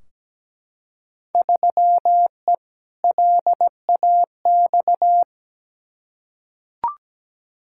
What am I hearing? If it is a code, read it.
Morse code, heard as a keyed tone, about 17 wpm, 700 Hz. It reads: 3E LAX
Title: Field Day Exchanges Encore - 17wpm